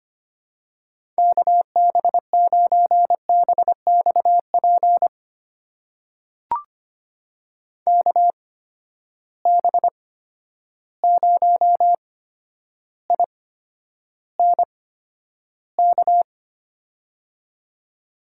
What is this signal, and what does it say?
Morse code, heard as a keyed tone, about 25 wpm, spaced as for 11 wpm, 700 Hz. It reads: KB9BXP KB0INK
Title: USA Call Signs - 6 Characters - 8x Character Spacing - 25wpm